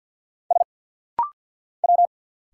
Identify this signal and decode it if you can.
Morse code, heard as a keyed tone, about 50 wpm, 700 Hz. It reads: S W
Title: Single Letters - Rapid-Fire 50wpm